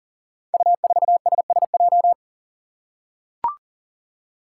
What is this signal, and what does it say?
Morse code, heard as a keyed tone, about 39 wpm, 700 Hz. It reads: U4SSJ
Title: Five Challenging Letter-Number Combinations 40wpm